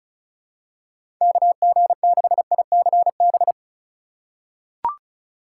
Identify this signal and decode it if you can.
Morse code, heard as a keyed tone, about 36 wpm, 690 Hz. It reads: KG6ICB
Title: USA Call Signs - 3x Character Spacing - 35wpm